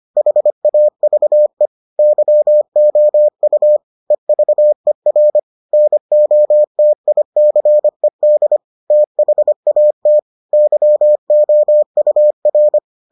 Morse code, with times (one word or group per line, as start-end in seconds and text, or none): 0.2-1.7 HAVE
2.0-3.8 YOU
4.1-5.4 EVER
5.7-8.6 NOTICED
8.9-10.2 THAT
10.5-12.8 YOUR